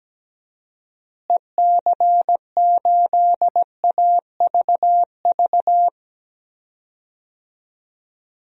Morse code, heard as a keyed tone, about 17 wpm, 700 Hz. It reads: EC8AVV
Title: DXCC-100 Call Signs - 6 Characters - Repeated 3x - 17wpm